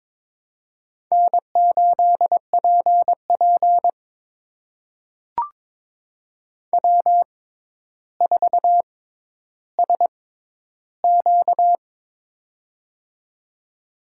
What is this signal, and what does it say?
Morse code, heard as a keyed tone, about 22 wpm, spaced as for 11 wpm, 700 Hz. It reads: N8PP W4SQ
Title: USA Call Signs - 4 Characters - 6x Character Spacing - 22wpm